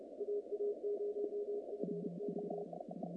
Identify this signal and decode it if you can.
Morse code, heard as a keyed tone, about 30 wpm, 440 Hz, with noise. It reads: AA0FO
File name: contests / naqp-cw-k5tr-op-k5pi- -2200